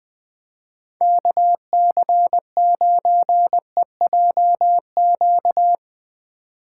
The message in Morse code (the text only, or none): KC9EJQ